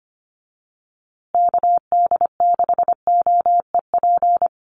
Morse code, heard as a keyed tone, about 25 wpm, 700 Hz. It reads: KD6OEP
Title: USA Call Signs Repeated 2x 25wpm